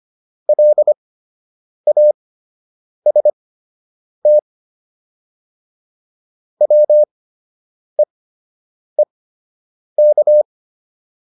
Morse code, heard as a keyed tone, about 25 wpm, spaced as for 8 wpm, 600 Hz. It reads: LAST WEEK